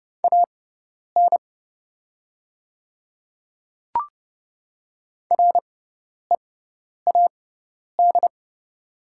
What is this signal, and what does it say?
Morse code, heard as a keyed tone, about 30 wpm, spaced as for 10 wpm, 700 Hz. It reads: AN READ
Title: 6x Character Spacing - Top 500 Words - 30wpm